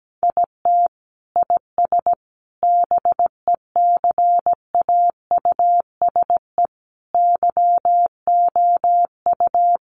Morse code, heard as a keyed tone, about 17 wpm, 700 Hz. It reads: IT IS BECAUSE YOU
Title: Sentences from Top 600 Words 17wpm